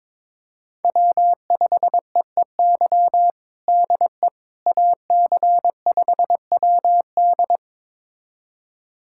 Morse code, heard as a keyed tone, about 22 wpm, 700 Hz. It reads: W5EEY DE AC5WD
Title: USA Callsign DE Callsign - 5 Characters - 22wpm